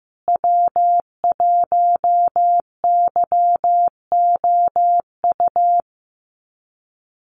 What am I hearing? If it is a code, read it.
Morse code, heard as a keyed tone, about 15 wpm, 700 Hz. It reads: W1YOU
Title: USA Call Signs Repeated 3x 15wpm